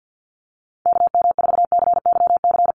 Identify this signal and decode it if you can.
Morse code, heard as a keyed tone, about 50 wpm, 700 Hz. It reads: KM4CYC
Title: USA Call Signs Repeated 3x 50wpm